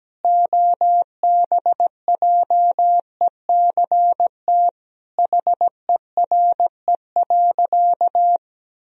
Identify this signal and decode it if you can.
Morse code, heard as a keyed tone, about 17 wpm, 700 Hz. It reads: OBJECT HERE.
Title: Sentences from Top 1000 Words - Review Part 3 - 17wpm